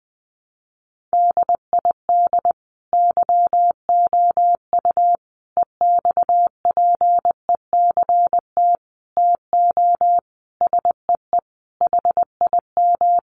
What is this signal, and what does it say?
Morse code, heard as a keyed tone, about 20 wpm, 700 Hz. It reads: DID YOU EXPECT TO SEE HIM
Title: Sentences from Top 1100 Words 20wpm